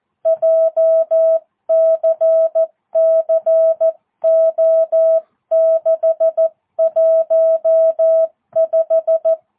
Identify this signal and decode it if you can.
Morse code, heard as a keyed tone, about 14 wpm, 650 Hz. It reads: JCCO615
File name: jh8cbh / 2022cw